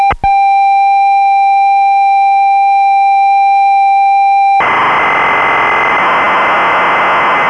Computer data on cassette.